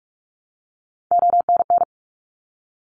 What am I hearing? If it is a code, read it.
Morse code, heard as a keyed tone, about 46 wpm, 700 Hz. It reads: ONN
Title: ARRL Sections 45wpm